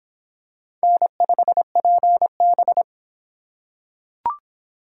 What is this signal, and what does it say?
Morse code, heard as a keyed tone, about 26 wpm, 700 Hz. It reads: N5PB